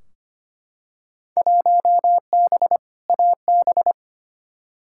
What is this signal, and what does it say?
Morse code, heard as a keyed tone, about 25 wpm, 700 Hz. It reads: 1B AB
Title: Field Day Exchanges - 25wpm